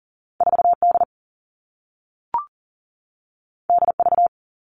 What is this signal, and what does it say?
Morse code, heard as a keyed tone, about 41 wpm, 700 Hz. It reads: 4D DV
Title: Two Challenging Letter-Number Combinations 40wpm